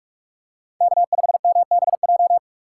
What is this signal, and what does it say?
Morse code, heard as a keyed tone, about 46 wpm, 700 Hz. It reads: K5MBJ
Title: USA Call Signs - 5 Characters - 2x Character Spacing - 45wpm